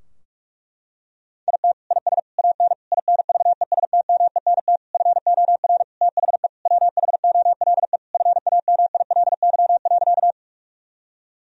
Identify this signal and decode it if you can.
Morse code, heard as a keyed tone, about 46 wpm, 700 Hz. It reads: IT IS AN INVESTMENT FOR THE WHOLE FAMILY.
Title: Sentences from Top 1000 Words - Review Part 4 - 45wpm